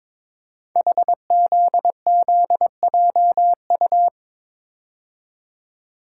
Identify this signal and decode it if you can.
Morse code, heard as a keyed tone, about 22 wpm, 700 Hz. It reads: HZZJU
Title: Code Groups - 5 Letters - 22wpm